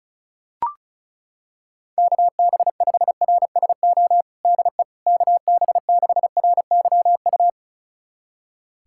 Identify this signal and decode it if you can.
Morse code, heard as a keyed tone, about 36 wpm, 700 Hz. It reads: KB5RSO DE KB6RYU
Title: USA Callsign DE Callsign - 6 Characters - 35wpm